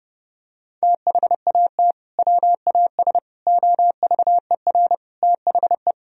Morse code, heard as a keyed tone, about 32 wpm, 700 Hz. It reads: THAT WAS OVER THE
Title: Sentences from Top 1000 Words - Review Part 2 - 30wpm